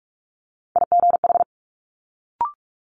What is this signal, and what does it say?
Morse code, heard as a keyed tone, about 46 wpm, 700 Hz. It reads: IGH
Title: Multi-letter Phonemes Quick Fire 45wpm